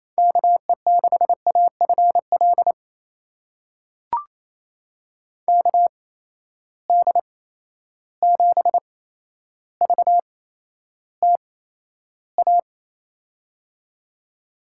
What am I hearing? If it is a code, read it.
Morse code, heard as a keyed tone, about 28 wpm, spaced as for 12 wpm, 700 Hz. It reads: KE6AFL KD7VTA
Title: USA Call Signs - 6 Characters - 8x Character Spacing - 28wpm